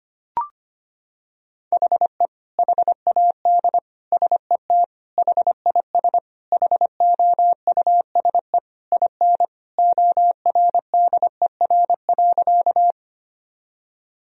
Morse code, heard as a keyed tone, about 25 wpm, 700 Hz. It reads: HE HAD SET HIS HOUSE IN ORDER.
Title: Sentences from Top 1000 Words - Review Part 2 - 25wpm